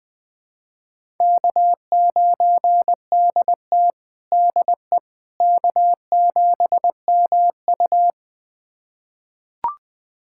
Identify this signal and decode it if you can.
Morse code, heard as a keyed tone, about 21 wpm, 700 Hz. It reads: K9DT DE K7MU
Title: USA Callsign DE Callsign - 4 Characters - 20wpm